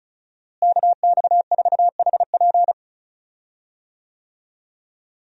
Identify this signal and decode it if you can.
Morse code, heard as a keyed tone, about 36 wpm, 700 Hz. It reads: KX4HP
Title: USA Call Signs - 5 Characters - Repeated 3x - 35wpm